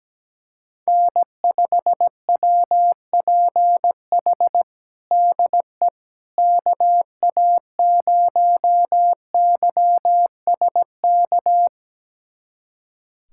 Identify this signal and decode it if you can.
Morse code, heard as a keyed tone, about 17 wpm, 700 Hz. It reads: N5WPH DE KA0YSK
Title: USA Callsign DE Callsign 17wpm